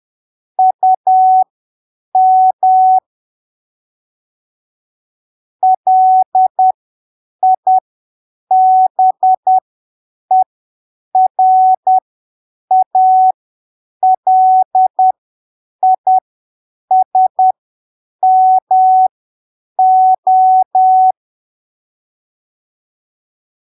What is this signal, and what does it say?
Morse code, heard as a keyed tone, about 10 wpm, spaced as for 7 wpm, 750 Hz. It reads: UM LIBERALISMO
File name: arquivos / texto / texto 05 10ppm